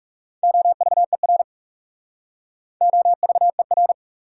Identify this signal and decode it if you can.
Morse code, heard as a keyed tone, about 41 wpm, 700 Hz. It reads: OVER OVER